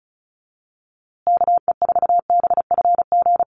Morse code, heard as a keyed tone, about 34 wpm, 700 Hz. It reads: KE4BFG
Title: USA Call Signs Repeated 3x 35wpm